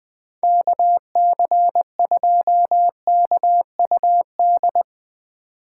Morse code, heard as a keyed tone, about 20 wpm, 700 Hz. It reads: KC2KUD